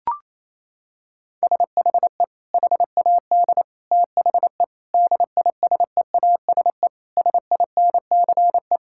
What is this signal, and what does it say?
Morse code, heard as a keyed tone, about 28 wpm, 700 Hz. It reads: SHE HAD THE DISEASE SINCE
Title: Sentences from Top 1000 Words - Review Part 6 - 28wpm